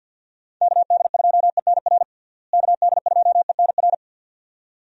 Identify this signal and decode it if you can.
Morse code, heard as a keyed tone, about 51 wpm, 700 Hz. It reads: KD2ENR KD2ENR
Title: USA Call Signs - 6 Characters - Repeated 2x - 50wpm